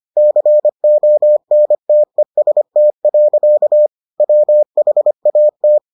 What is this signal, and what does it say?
Morse code, heard as a keyed tone, about 25 wpm, 600 Hz. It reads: CONTEST. WHAT